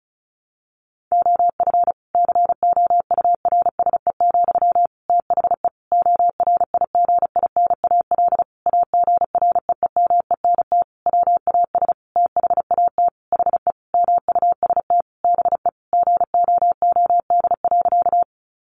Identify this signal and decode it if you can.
Morse code, heard as a keyed tone, about 34 wpm, 700 Hz. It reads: OF COURSE, THE ORIGINAL AGREEMENT WAS THAT HE MUST BE GOOD.
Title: Sentences from Top 1500 Words 35wpm